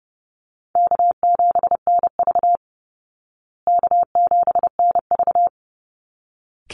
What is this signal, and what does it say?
Morse code, heard as a keyed tone, about 30 wpm, 700 Hz. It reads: K7NV K7NV
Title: Easy Call Signs Repeated 2x - 30wpm